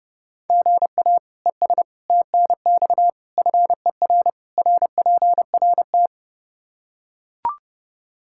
Morse code, heard as a keyed tone, about 32 wpm, 700 Hz. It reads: GA ES TNX FER RPRT
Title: K1USN SST QSO Practice 30wpm